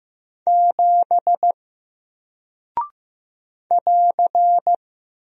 Morse code, heard as a keyed tone, about 15 wpm, 700 Hz. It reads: 7 <AR>
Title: Single Character (Letter:Number:Punctuation:Prosign) Alt 15wpm